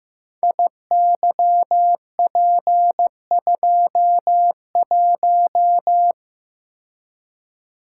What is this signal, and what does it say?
Morse code, heard as a keyed tone, about 15 wpm, 700 Hz. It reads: IYP21
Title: Code Groups - 5 Letter:Numbers - 15wpm